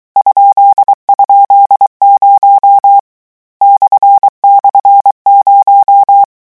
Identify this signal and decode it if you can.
Morse code, heard as a keyed tone, about 23 wpm, 800 Hz. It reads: ??0 //0
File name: morse mp3 / lecon13-vitesse 22